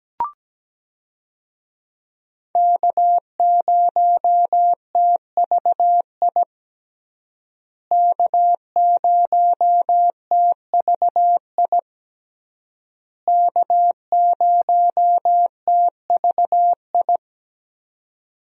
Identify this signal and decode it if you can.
Morse code, heard as a keyed tone, about 17 wpm, 700 Hz. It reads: K0TVI K0TVI K0TVI
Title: USA Call Signs - 5 Characters - Repeated 3x - 17wpm